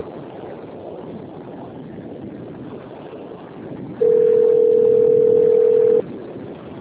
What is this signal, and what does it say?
Ringback tone (North America), 440 + 480 Hz.
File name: out- - - - - .92205